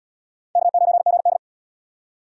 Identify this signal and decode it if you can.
Morse code, heard as a keyed tone, about 77 wpm, 700 Hz. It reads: K9GD